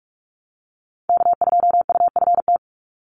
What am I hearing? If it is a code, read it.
Morse code, heard as a keyed tone, about 44 wpm, 700 Hz. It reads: K2UFT
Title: Call Signs - 45wpm